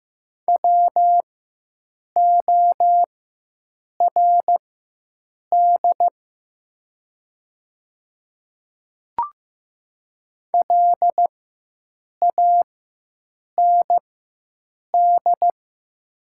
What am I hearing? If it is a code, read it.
Morse code, heard as a keyed tone, about 15 wpm, spaced as for 7 wpm, 700 Hz. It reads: WORD LAND
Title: 4x Character Spacing - Top 200 Words - 15wpm